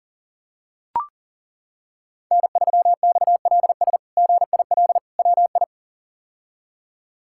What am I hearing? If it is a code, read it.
Morse code, heard as a keyed tone, about 40 wpm, 700 Hz. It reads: N3XLS GIL WI
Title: K1USN SST QSO Practice - 40wpm